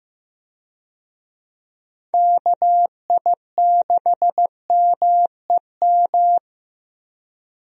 Morse code, heard as a keyed tone, about 15 wpm, 700 Hz. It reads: KI6MEM
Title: USA Call Signs - 6 Characters - Repeated 3x - 15wpm